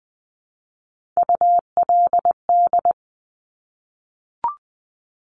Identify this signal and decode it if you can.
Morse code, heard as a keyed tone, about 20 wpm, 700 Hz. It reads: ULD
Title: Top Three Letter Combinations in Words 20wpm